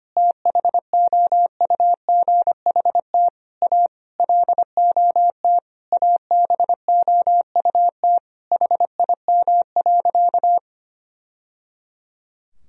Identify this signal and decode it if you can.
Morse code, heard as a keyed tone, about 25 wpm, 700 Hz. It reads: THOUGHT A LOT ABOUT HIM.
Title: Sentences from Top 1000 Words - Review Part 3 - 25wpm